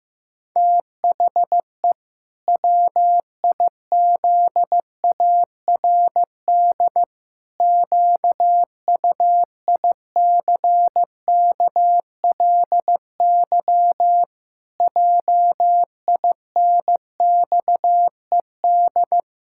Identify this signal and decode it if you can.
Morse code, heard as a keyed tone, about 15 wpm, 700 Hz. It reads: THE WIZARD QUICKLY JINXED
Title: Pangram Challenge v2 - 15wpm